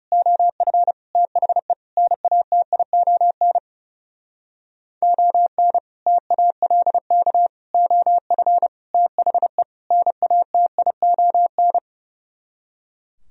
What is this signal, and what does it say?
Morse code, heard as a keyed tone, about 37 wpm, 700 Hz. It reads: OF THE NATION ON TALK OF THE NATION
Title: Speed-Racing - Sets of 5 Words 30wpm